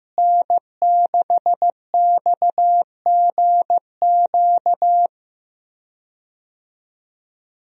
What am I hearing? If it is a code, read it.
Morse code, heard as a keyed tone, about 15 wpm, 700 Hz. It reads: N6XGQ
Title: USA Call Signs - 5 Characters - Repeated 3x - 15wpm